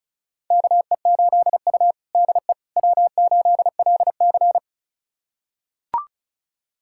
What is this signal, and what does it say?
Morse code, heard as a keyed tone, about 36 wpm, 700 Hz. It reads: KE8U DE W8LC
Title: USA Callsign DE Callsign - 4 Characters - 35wpm